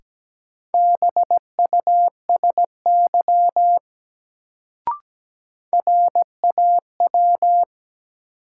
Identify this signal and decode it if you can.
Morse code, heard as a keyed tone, about 17 wpm, 700 Hz. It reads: BUSY RAW
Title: Sentences from Top 1500 Words - New Word Drill Alt - 17wpm